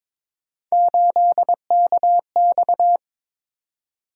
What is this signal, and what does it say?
Morse code, heard as a keyed tone, about 22 wpm, 700 Hz. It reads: 8KX
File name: practice / 22wpm / Three Letter-Number Combinations - 22wpm